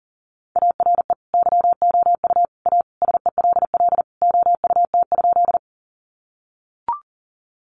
Morse code, heard as a keyed tone, about 40 wpm, 700 Hz. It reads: ARE YOU A SELL OUT?
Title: Sentences from Top 1100 Words 40wpm